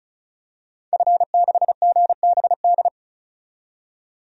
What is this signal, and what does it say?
Morse code, heard as a keyed tone, about 36 wpm, 700 Hz. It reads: F6GBD
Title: DXCC-100 Call Signs - 5 Characters - 35wpm